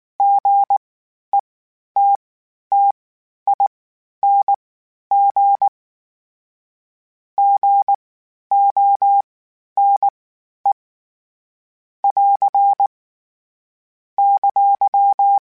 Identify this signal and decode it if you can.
Morse code, heard as a keyed tone, about 19 wpm, spaced as for 11 wpm, 800 Hz. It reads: GETTING GONE <AR> !